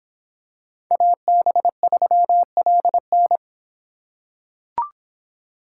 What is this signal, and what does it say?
Morse code, heard as a keyed tone, about 26 wpm, 700 Hz. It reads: AB3LN